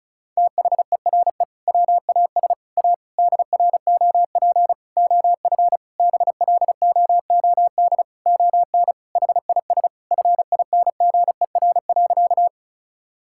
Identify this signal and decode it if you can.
Morse code, heard as a keyed tone, about 36 wpm, 700 Hz. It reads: THERE WAS A DROP OF BLOOD ON HIS FINGER.
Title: Sentences from Top 1600 Words 35wpm